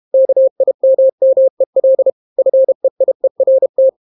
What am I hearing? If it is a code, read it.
Morse code, heard as a keyed tone, about 32 wpm, 530 Hz. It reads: KIMMEL FEIERT